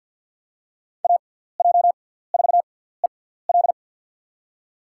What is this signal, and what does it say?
Morse code, heard as a keyed tone, about 52 wpm, spaced as for 17 wpm, 700 Hz. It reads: AJ4EL
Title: USA Call Signs - 5 Characters - 6x Character Spacing - 50wpm